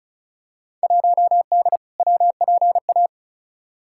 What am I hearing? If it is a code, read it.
Morse code, heard as a keyed tone, about 36 wpm, 700 Hz. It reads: 1D WPA
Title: Field Day Exchanges - 35wpm